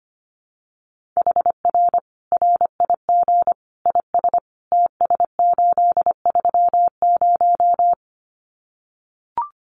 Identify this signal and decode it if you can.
Morse code, heard as a keyed tone, about 25 wpm, 700 Hz. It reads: HR RIG IS TS830
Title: Common QSO Phrases 25wpm